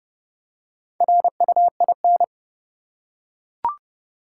Morse code, heard as a keyed tone, about 30 wpm, 700 Hz. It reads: RUIN
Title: Four Letter Words - 30wpm